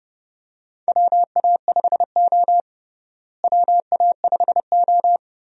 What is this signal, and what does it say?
Morse code, heard as a keyed tone, about 32 wpm, 700 Hz. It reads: WA5O WA5O
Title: USA Call Signs - 4 Characters - Repeated 2x - 30wpm